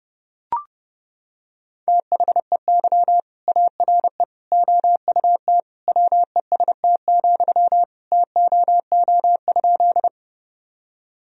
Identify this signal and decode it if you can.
Morse code, heard as a keyed tone, about 30 wpm, 700 Hz. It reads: THEY ARE OUT WEST, TOO?
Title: Sentences from Top 1000 Words - Review Part 2 - 30wpm